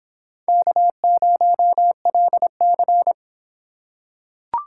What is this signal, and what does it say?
Morse code, heard as a keyed tone, about 25 wpm, 700 Hz. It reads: K0LC